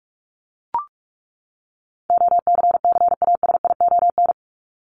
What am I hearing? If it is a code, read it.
Morse code, heard as a keyed tone, about 44 wpm, 700 Hz. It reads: OCCASION
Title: Top 1501-1750 Words 45wpm